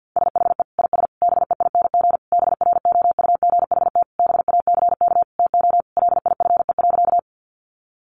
Morse code, heard as a keyed tone, about 50 wpm, 700 Hz. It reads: SHE IS BEING BROUGHT BACK TO LIFE.
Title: Sentences from Top 600 Words 50wpm